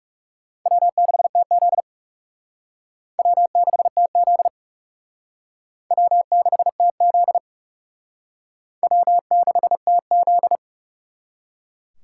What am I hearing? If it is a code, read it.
Morse code, heard as a keyed tone, about 36 wpm, 700 Hz. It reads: W6TZ W6TZ W6TZ W6TZ
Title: Speed Racing - USA 1x2 Call Signs 30wpm